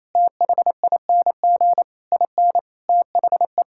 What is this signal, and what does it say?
Morse code, heard as a keyed tone, about 28 wpm, 700 Hz. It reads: THING IN THE
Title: Sentences from Top 1000 Words - Review Part 6 - 28wpm